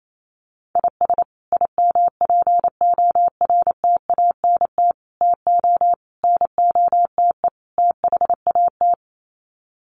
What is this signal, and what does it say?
Morse code, heard as a keyed tone, about 28 wpm, 700 Hz. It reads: IS IMPORTANT TO NOTE THAT
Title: Sets of 5 Words 28wpm